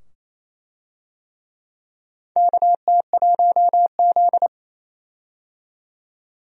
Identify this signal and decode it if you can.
Morse code, heard as a keyed tone, about 28 wpm, 700 Hz. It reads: KT1Z
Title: USA Call Signs - 4 Characters - 28wpm